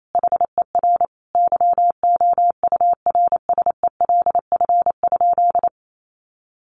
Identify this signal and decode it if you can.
Morse code, heard as a keyed tone, about 28 wpm, 700 Hz. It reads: HER YOURSELF?
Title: Sentences from Top 1500 Words 28wpm